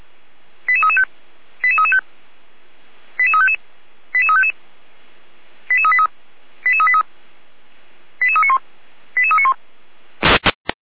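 Five-tone selective call (ZVEI), heard as quick runs of five tones, tones of about 70 ms. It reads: 88335 88335 88355 88355 88383 88383 88391 88391
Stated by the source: ZVEI five-tone call 88367 received by SDR.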